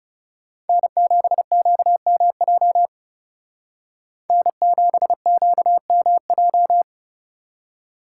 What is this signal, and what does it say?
Morse code, heard as a keyed tone, about 34 wpm, 700 Hz. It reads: N7QMJ N7QMJ
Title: Speed Racing - USA 1x3 Call Signs 30wpm